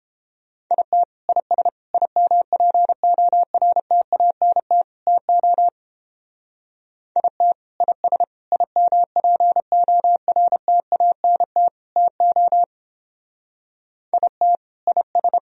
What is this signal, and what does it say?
Morse code, heard as a keyed tone, about 29 wpm, 700 Hz. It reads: IT IS IMPORTANT TO IT IS IMPORTANT TO IT IS